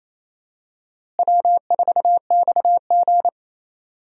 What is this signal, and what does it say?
Morse code, heard as a keyed tone, about 28 wpm, 700 Hz. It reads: W4XG
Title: USA Call Signs - 4 Characters - Repeated 2x - 28wpm